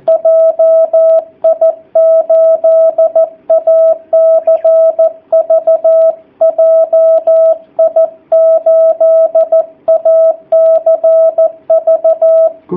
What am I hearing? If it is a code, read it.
Morse code, heard as a keyed tone, about 14 wpm, 660 Hz. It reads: JI8ACV JI8ACV
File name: jh8cbh / 2019cw / sp13- w